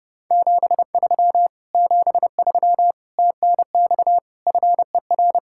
Morse code, heard as a keyed tone, about 33 wpm, 700 Hz. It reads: 73 73 TNX FER